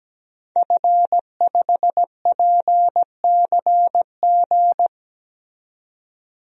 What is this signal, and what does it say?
Morse code, heard as a keyed tone, about 17 wpm, 700 Hz. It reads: F5PCG
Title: DXCC-100 Call Signs - 5 Characters - 17wpm